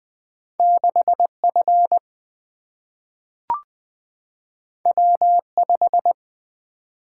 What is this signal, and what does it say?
Morse code, heard as a keyed tone, about 20 wpm, 700 Hz. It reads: 6F W5
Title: Single-Letter Single-Number 20wpm